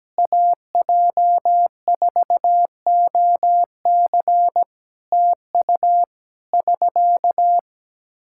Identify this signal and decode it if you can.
Morse code, heard as a keyed tone, about 17 wpm, 700 Hz. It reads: AJ4OC TU <SK>
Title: Practice QSO - Exchange 4 - Ending - 17wpm